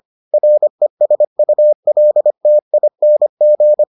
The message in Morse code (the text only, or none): RESULTING